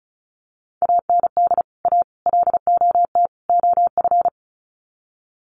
Morse code, heard as a keyed tone, about 34 wpm, 700 Hz. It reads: AND A LOT OF
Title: Sets of 4 Words 35wpm